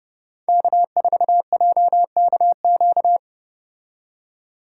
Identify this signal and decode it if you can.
Morse code, heard as a keyed tone, about 32 wpm, 700 Hz. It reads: K4JKQ